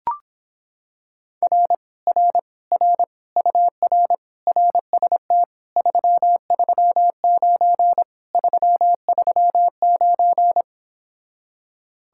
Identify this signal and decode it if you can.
Morse code, heard as a keyed tone, about 26 wpm, 700 Hz. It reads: R R R UR RST 339 339